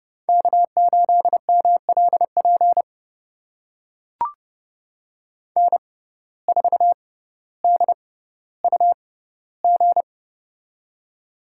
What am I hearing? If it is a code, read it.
Morse code, heard as a keyed tone, about 31 wpm, spaced as for 16 wpm, 700 Hz. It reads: K8MLP N4DUG
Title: USA Call Signs - 5 Characters - 6x Character Spacing - 30wpm